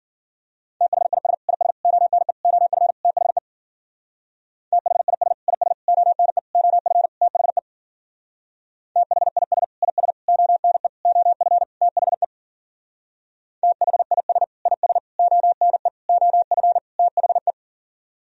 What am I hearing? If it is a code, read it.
Morse code, heard as a keyed tone, about 52 wpm, 700 Hz. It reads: THIS IS ONE OF THE THIS IS ONE OF THE THIS IS ONE OF THE THIS IS ONE OF THE